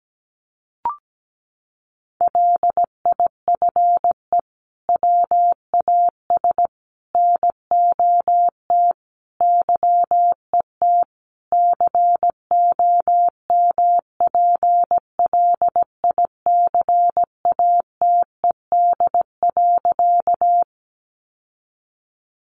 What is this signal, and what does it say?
Morse code, heard as a keyed tone, about 17 wpm, 700 Hz. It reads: LIFE WAS NOT YET COMPLICATED.
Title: Sentences from Top 1200 Words 17wpm